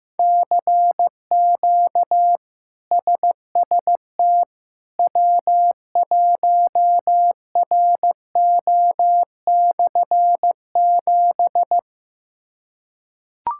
Morse code, heard as a keyed tone, about 16 wpm, 700 Hz. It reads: CQ SST W1RO/7
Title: K1USN SST QSO Practice - 15wpm